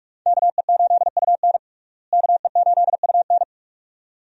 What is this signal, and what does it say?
Morse code, heard as a keyed tone, about 43 wpm, 700 Hz. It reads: KE8UN KE8UN
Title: USA Call Signs - 5 Characters - Repeated 2x - 45wpm